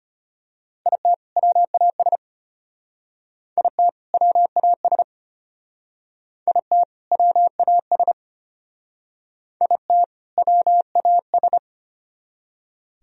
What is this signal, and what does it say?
Morse code, heard as a keyed tone, about 31 wpm, 700 Hz. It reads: IT WAS IT WAS IT WAS IT WAS
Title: Speed-Racing - Sets of 2 Words 25wpm